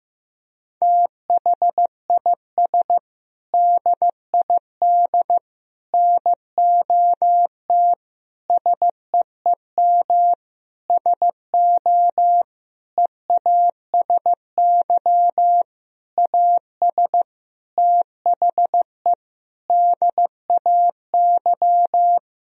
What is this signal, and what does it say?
Morse code, heard as a keyed tone, about 15 wpm, 700 Hz. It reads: THIS DID NOT SEEM SO EASY AS THE DAY
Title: Sentences from Top 1000 Words - Review Part 1 - 15wpm